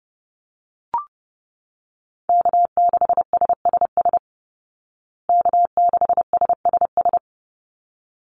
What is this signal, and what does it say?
Morse code, heard as a keyed tone, about 30 wpm, 700 Hz. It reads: K6SSS K6SSS
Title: Call Signs Repeated 3x - 30wpm